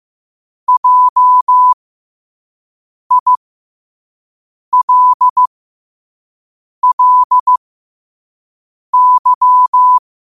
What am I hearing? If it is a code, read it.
Morse code, heard as a keyed tone, about 15 wpm, spaced as for 5 wpm, 1000 Hz. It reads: JILLY
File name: qotdfiles / 5wpm